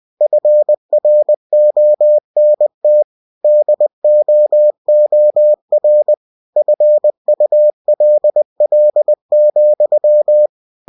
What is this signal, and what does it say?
Morse code, heard as a keyed tone, about 20 wpm, 600 Hz. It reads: FRONT DOOR FULL,
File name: cw-books / winnie-the-pooh / winnie-20wpm-page-0016